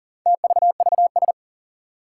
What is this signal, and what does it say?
Morse code, heard as a keyed tone, about 42 wpm, 700 Hz. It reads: TVVS
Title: Four Challenging Letter-Number Combinations 40wpm